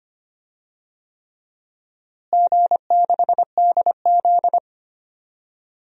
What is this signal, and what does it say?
Morse code, heard as a keyed tone, about 25 wpm, 700 Hz. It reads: G6DZ
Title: DXCC-100 Call Signs - 4 Characters - Repeated 3x - 25wpm